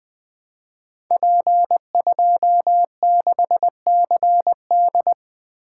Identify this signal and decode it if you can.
Morse code, heard as a keyed tone, about 20 wpm, 700 Hz. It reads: P26CD
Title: Code Groups - 5 Letter:Numbers - 20wpm